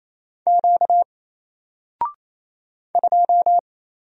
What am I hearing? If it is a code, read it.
Morse code, heard as a keyed tone, about 27 wpm, 700 Hz. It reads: Q 2
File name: Single-Letter-Number-28wpm